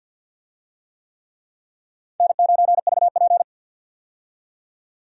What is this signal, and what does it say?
Morse code, heard as a keyed tone, about 47 wpm, 700 Hz. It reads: N9VP
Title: USA Call Signs - 4 Characters - 50wpm